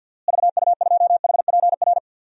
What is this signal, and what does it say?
Morse code, heard as a keyed tone, about 52 wpm, 700 Hz. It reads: VU2HPR
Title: DXCC-100 Call Signs - 6 Characters - Repeated 2x - 50wpm